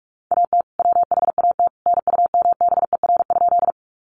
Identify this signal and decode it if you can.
Morse code, heard as a keyed tone, about 44 wpm, 700 Hz. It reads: AT WHAT NUMBER?
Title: Sentences from Top 100 Words 45wpm